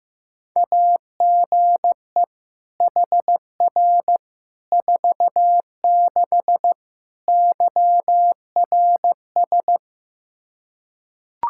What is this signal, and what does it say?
Morse code, heard as a keyed tone, about 15 wpm, 700 Hz. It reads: AGE HR 46 YRS
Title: Practice QSO - Exchange 3 - More Info - 15wpm